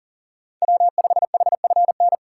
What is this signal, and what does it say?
Morse code, heard as a keyed tone, about 40 wpm, 700 Hz. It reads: W5HFN
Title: USA Call Signs - 4x Character Spacing - 40wpm